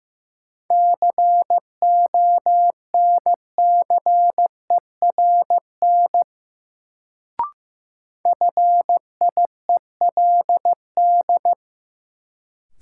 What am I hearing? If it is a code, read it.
Morse code, heard as a keyed tone, about 15 wpm, 700 Hz. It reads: CONCERN FIELD
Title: Sentences from Top 1000 Words - New Word Drill - 15wpm